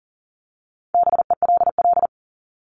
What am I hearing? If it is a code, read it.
Morse code, heard as a keyed tone, about 40 wpm, 700 Hz. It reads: BELL
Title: Top 1751-2000 Words 40wpm